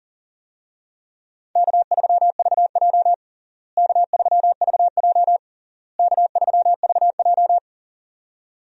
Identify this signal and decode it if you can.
Morse code, heard as a keyed tone, about 40 wpm, 700 Hz. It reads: K3VJ K3VJ K3VJ
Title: USA Call Signs - 4 Characters - Repeated 3x - 40wpm